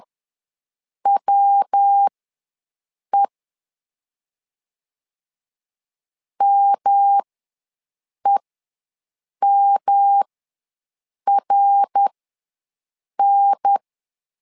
Morse code, heard as a keyed tone, about 10 wpm, spaced as for 6 wpm, 800 Hz. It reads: WE MEMRN
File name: LEZIONI / 3 MWRO / 3 MWRO ws 10wpm 2310